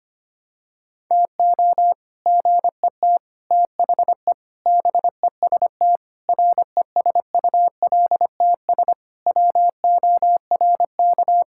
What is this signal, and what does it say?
Morse code, heard as a keyed tone, about 25 wpm, 700 Hz. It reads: TO GET THE BEST RESULTS WORK
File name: practice / 25wpm / Sentences with 3-Word Phrases - 25wpm